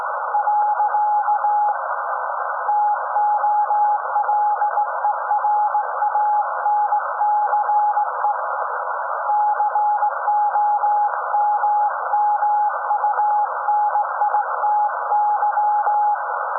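Morse code, heard as a keyed tone, about 18 wpm, 840 Hz. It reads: . TOLERANCE COMMUNITY
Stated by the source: SAQ Grimeton Radio, Alexanderson 200 kW alternator on 17.2 kHz, received in Potsdam with a very strong signal on World Radio Day.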